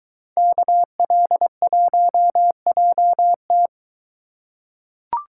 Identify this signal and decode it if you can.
Morse code, heard as a keyed tone, about 23 wpm, 700 Hz. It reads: KL1JT